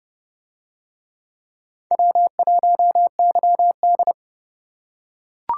Morse code, heard as a keyed tone, about 29 wpm, 700 Hz. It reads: W1YD